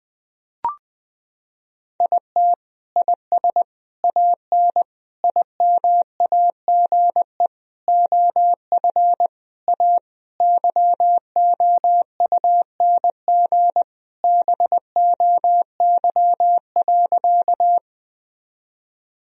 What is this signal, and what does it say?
Morse code, heard as a keyed tone, about 21 wpm, 700 Hz. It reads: IT IS AN IMAGE OF A YOUNG BOY.
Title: Sentences from Top 1000 Words - Review Part 4 - 20wpm